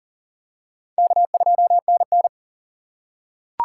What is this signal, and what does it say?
Morse code, heard as a keyed tone, about 40 wpm, 700 Hz. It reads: K2NN
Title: USA Call Signs - 4 Characters - 6x Character Spacing - 40wpm